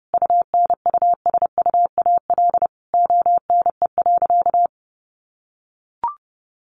Morse code, heard as a keyed tone, about 30 wpm, 700 Hz. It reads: UNUSUAL ONE.
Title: Sentences from Top 1400 Words 30wpm